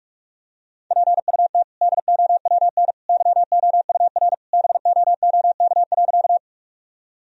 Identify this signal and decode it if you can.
Morse code, heard as a keyed tone, about 46 wpm, 700 Hz. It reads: PUT DOWN YOUR BOOK.
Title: Sentences from Top 1000 Words - Review Part 1 - 45wpm